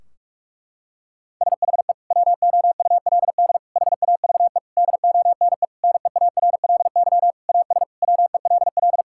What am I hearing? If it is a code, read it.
Morse code, heard as a keyed tone, about 46 wpm, 700 Hz. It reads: SHE WOULD HAVE DONE NEARLY AS WELL
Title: Sentences from Top 1600 Words 45wpm